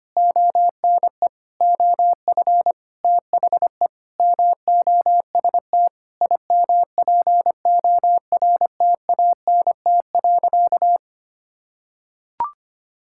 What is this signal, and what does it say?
Morse code, heard as a keyed tone, about 25 wpm, 700 Hz. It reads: ONE OF THE MOST IMPORTANT.
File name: practice / 25wpm / Sentences with 9-Word Phrases - 25wpm